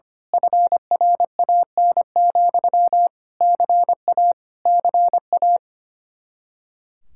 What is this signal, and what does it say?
Morse code, heard as a keyed tone, about 24 wpm, 700 Hz. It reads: FRAN, CA CA
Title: Speed-Racing Common QSO Phrases - 25wpm